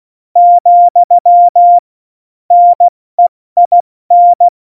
Morse code, heard as a keyed tone, about 16 wpm, spaced as for 14 wpm, 700 Hz. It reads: , NEIN
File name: Morsekurs / Witze / MP3, Witz 08. Mai 2018, 70 F80